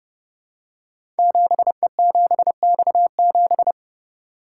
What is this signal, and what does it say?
Morse code, heard as a keyed tone, about 29 wpm, 700 Hz. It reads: 7E7X7
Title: Code Groups - 5 Letter:Numbers - 30wpm